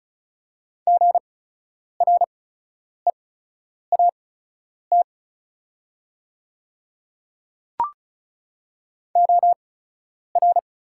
Morse code, heard as a keyed tone, about 36 wpm, spaced as for 10 wpm, 690 Hz. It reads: GREAT OR
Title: 8x Character Spacing - Top 200 Words - 35wpm